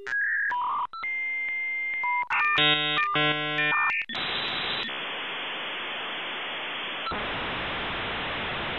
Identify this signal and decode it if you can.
Dial-up modem handshake.
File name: B8-Old Modem Sounds (300 baud to 56K)